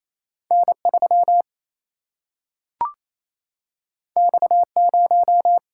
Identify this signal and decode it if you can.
Morse code, heard as a keyed tone, about 28 wpm, 700 Hz. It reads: N3 X0
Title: Single-Letter Single-Number 28wpm